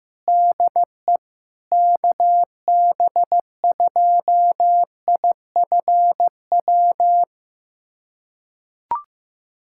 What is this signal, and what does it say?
Morse code, heard as a keyed tone, about 16 wpm, 700 Hz. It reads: DE KB2IFW
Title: USA Callsign DE Callsign 15wpm